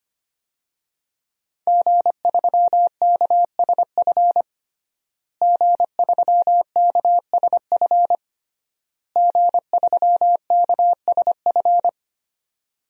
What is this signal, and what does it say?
Morse code, heard as a keyed tone, about 25 wpm, 700 Hz. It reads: G3KSF G3KSF G3KSF
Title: DXCC-100 Call Signs - 5 Characters - Repeated 3x - 25wpm